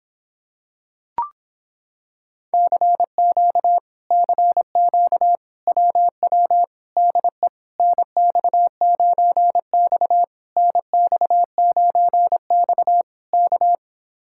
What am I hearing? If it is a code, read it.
Morse code, heard as a keyed tone, about 26 wpm, 700 Hz. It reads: CQ CQ WW DE NX9X NX9X K